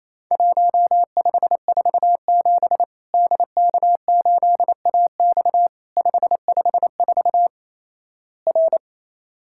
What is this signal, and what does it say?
Morse code, heard as a keyed tone, about 28 wpm, 700 Hz. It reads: 1547 DK8AX 554 R
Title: Worked All Europe DX Contest - QTC Exchange - 28wpm